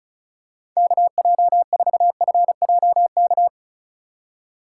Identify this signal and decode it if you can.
Morse code, heard as a keyed tone, about 36 wpm, 700 Hz. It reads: KJ4FJK
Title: USA Call Signs - 4x Character Spacing - 35wpm